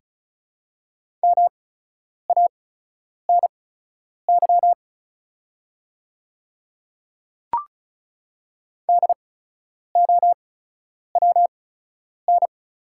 Morse code, heard as a keyed tone, about 36 wpm, spaced as for 10 wpm, 700 Hz. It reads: MANY DOWN
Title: 8x Character Spacing - Top 100 Words - 35wpm